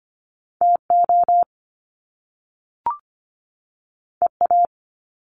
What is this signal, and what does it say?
Morse code, heard as a keyed tone, about 25 wpm, 700 Hz. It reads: TO EA
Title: Top Two Letter Combinations in Words 25wpm